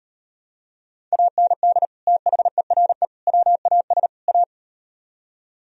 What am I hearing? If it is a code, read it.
Morse code, heard as a keyed tone, about 38 wpm, 700 Hz. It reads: AND THERE WAS A